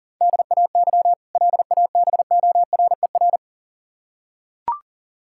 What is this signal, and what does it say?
Morse code, heard as a keyed tone, about 40 wpm, 700 Hz. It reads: DAY LABORER
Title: K1USN SST QSO Practice 40wpm